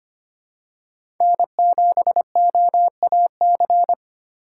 Morse code, heard as a keyed tone, about 25 wpm, 700 Hz. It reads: N7OAC
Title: USA Call Signs - 5 Characters - 2x Character Spacing - 25wpm